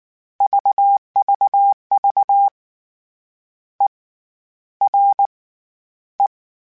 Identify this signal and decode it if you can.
Morse code, heard as a keyed tone, about 19 wpm, spaced as for 11 wpm, 800 Hz. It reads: VVV ERE
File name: General / General-mp3s / ERE @18